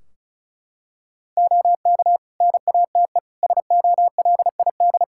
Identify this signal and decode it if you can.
Morse code, heard as a keyed tone, about 36 wpm, 700 Hz. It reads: OK NATE SOLID
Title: Practice QSO - Exchange 3 - More Info - 35wpm